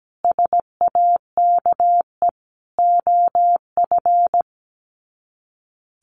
Morse code, heard as a keyed tone, about 17 wpm, 700 Hz. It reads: SAKE OF
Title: Sets of 4 Words 17wpm